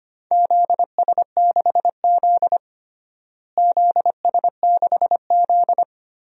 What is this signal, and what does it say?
Morse code, heard as a keyed tone, about 25 wpm, 700 Hz. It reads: ZS6Z ZS6Z
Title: DXCC-100 Call Signs - 4 Characters - Repeated 2x - 25wpm